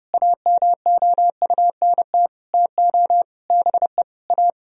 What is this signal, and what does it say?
Morse code, heard as a keyed tone, about 32 wpm, 700 Hz. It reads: AMOUNT TO BE A
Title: Sentences from Top 1000 Words - Review Part 4 - 30wpm